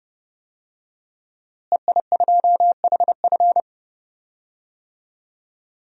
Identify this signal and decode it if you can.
Morse code, heard as a keyed tone, about 30 wpm, 700 Hz. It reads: EI2HF
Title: DXCC-100 Call Signs - 5 Characters - Repeated 2x - 30wpm